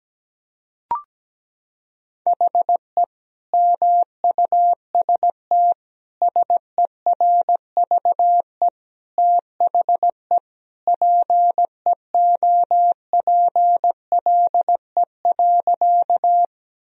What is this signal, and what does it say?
Morse code, heard as a keyed tone, about 17 wpm, 700 Hz. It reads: HE MUST SERVE THE PEOPLE.
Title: Sentences from Top 1000 Words - Review Part 2 - 17wpm